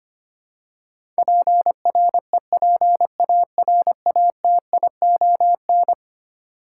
Morse code, heard as a keyed tone, about 25 wpm, 700 Hz. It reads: PREPARATION
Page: morse-code-ninja-practice-25wpm